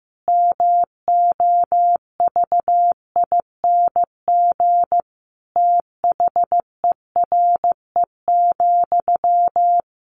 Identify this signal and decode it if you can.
Morse code, heard as a keyed tone, about 15 wpm, 700 Hz. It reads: MOVING THERE,
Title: A Sound Of Thunder 15wpm